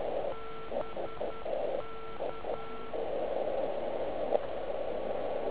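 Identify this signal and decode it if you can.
Morse code, heard as a keyed tone, about 10 wpm, 1340 Hz. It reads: BK
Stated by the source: Slow (QRS) CW test by beginner OK1IFW during beginner Morse skeds on 3559.5 kHz (80 m).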